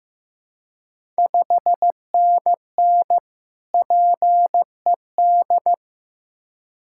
Morse code, heard as a keyed tone, about 15 wpm, 700 Hz. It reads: 5NN PED
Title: Ontario QSO Party Exchange 15wpm